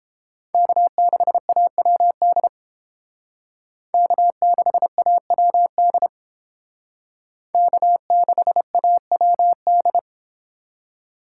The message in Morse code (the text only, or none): K6AWD K6AWD K6AWD